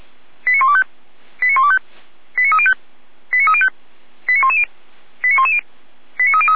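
Five-tone selective call (ZVEI), heard as quick runs of five tones, tones of about 70 ms. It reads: 89125 89125 89305 89305 89110 89110 89303
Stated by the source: ZVEI five-tone selective call received by SDR, from feuerwehr-orlamuende.org (fire brigade Orlamünde).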